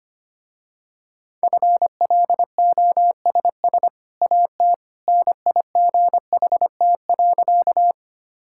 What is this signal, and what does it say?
Morse code, heard as a keyed tone, about 25 wpm, 700 Hz. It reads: FLOSS AT NIGHT.
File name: Easy Sentences with Hints - 25wpm